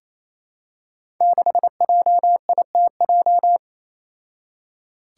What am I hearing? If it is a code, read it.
Morse code, heard as a keyed tone, about 28 wpm, 700 Hz. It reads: 6JITJ
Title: Five Challenging Letter-Number Combinations 28wpm